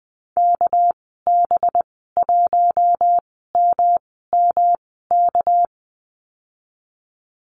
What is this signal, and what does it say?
Morse code, heard as a keyed tone, about 20 wpm, spaced as for 15 wpm, 700 Hz. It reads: KB1MMK
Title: 207 Lesson 33g - Letter j - Call Signs 20wpm